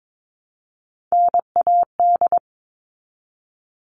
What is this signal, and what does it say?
Morse code, heard as a keyed tone, about 22 wpm, 700 Hz. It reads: NAD
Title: All Cut Numbers - 3-Digit 22wpm